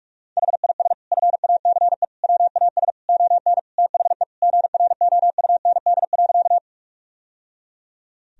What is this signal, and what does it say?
Morse code, heard as a keyed tone, about 46 wpm, 700 Hz. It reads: HIS FACE WAS ON THE GROUND.
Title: Sentences from Top 1000 Words - Review Part 2 - 45wpm